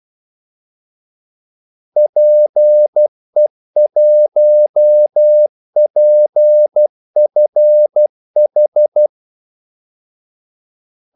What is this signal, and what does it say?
Morse code, heard as a keyed tone, about 12 wpm, 600 Hz. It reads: PE1PFH
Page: 2 ekstra ordmellemrum mellem kaldesignalerne